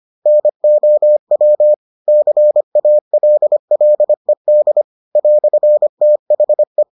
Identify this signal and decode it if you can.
Morse code, heard as a keyed tone, about 25 wpm, 600 Hz. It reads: NOW CALLED "THE